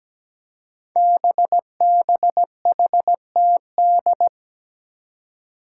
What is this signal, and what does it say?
Morse code, heard as a keyed tone, about 17 wpm, 700 Hz. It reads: BBHTD
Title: Five Challenging Letter-Number Combinations 17wpm